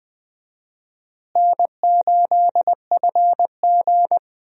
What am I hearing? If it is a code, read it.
Morse code, heard as a keyed tone, about 20 wpm, 700 Hz. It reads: N8FG
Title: USA Call Signs - 4 Characters - Repeated 2x - 20wpm